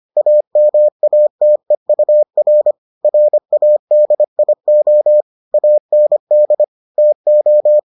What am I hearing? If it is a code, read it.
Morse code, heard as a keyed tone, about 25 wpm, 600 Hz. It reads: AMATEUR RADIO AND TO